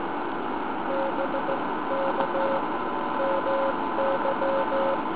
Morse code, heard as a keyed tone, about 16 wpm, spaced as for 14 wpm, 530 Hz. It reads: BK MY